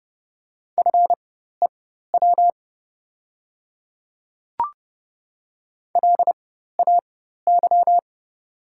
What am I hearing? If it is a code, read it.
Morse code, heard as a keyed tone, about 30 wpm, spaced as for 14 wpm, 700 Hz. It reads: FEW LAY
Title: 4x Character Spacing - Top 500 Words - 30wpm